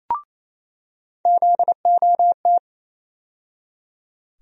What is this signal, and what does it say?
Morse code, heard as a keyed tone, about 28 wpm, 700 Hz. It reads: ZOT
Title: USA Call Sign Postfixes - 3 Letters 28wpm